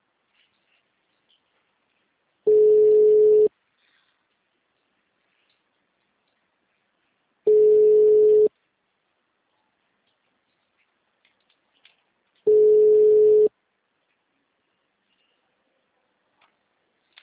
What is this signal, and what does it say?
Ringback tone (Europe, 425 Hz).